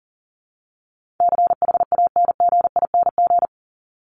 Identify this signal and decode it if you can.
Morse code, heard as a keyed tone, about 39 wpm, 700 Hz. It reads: CHANGING
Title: ING Suffix - Top 100 Words with Base Words 40wpm